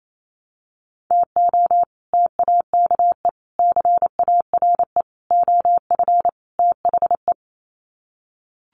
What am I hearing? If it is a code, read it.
Morse code, heard as a keyed tone, about 28 wpm, 700 Hz. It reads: TO TAKE CARE OF THE
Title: Sets of 5 Words Encore 28wpm